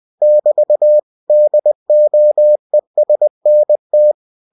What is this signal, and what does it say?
Morse code, heard as a keyed tone, about 20 wpm, 600 Hz. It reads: <BT> DOESNT
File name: cw-books / winnie-the-pooh / winnie-20wpm-page-0058